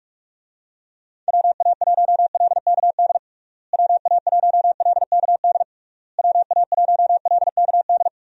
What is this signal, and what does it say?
Morse code, heard as a keyed tone, about 46 wpm, 700 Hz. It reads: WA1LKD WA1LKD WA1LKD
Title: USA Call Signs - 6 Characters - Repeated 3x - 45wpm